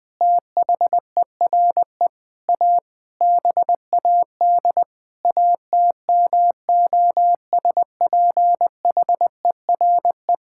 Morse code, heard as a keyed tone, about 20 wpm, 700 Hz. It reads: THERE A BAD ATMOSPHERE
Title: Sentences from Top 1600 Words 20wpm